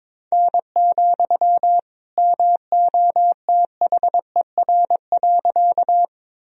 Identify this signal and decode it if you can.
Morse code, heard as a keyed tone, about 21 wpm, 700 Hz. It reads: N, MOTHER.
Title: Sentences from Top 200 Words and Abbreviations 22wpm